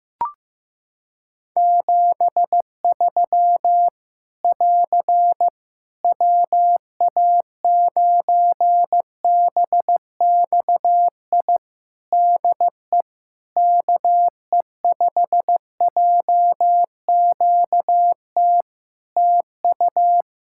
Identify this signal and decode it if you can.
Morse code, heard as a keyed tone, about 15 wpm, 700 Hz. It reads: 73 <AR> WA9BXI DE KE5JQT TU
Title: Practice QSO - Exchange 4 - Ending - 15wpm